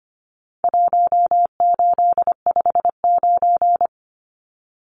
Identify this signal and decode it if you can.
Morse code, heard as a keyed tone, about 25 wpm, 700 Hz. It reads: 1859
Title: Numbers - 4-Digit 25wpm